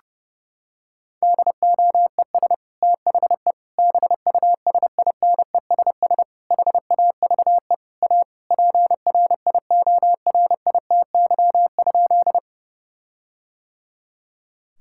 Morse code, heard as a keyed tone, about 30 wpm, 700 Hz. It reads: DOES THE BUSINESS HAVE A PRIORITY?
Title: Sentences from Top 1600 Words 30wpm